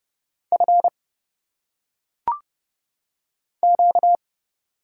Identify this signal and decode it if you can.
Morse code, heard as a keyed tone, about 31 wpm, 700 Hz. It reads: F Q
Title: Single Letters - Phonetic Pronunciation 30wpm